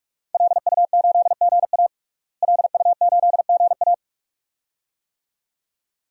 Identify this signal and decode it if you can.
Morse code, heard as a keyed tone, about 46 wpm, 700 Hz. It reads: LU8GA LU8GA
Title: DXCC-100 Call Signs - 5 Characters - Repeated 2x - 45wpm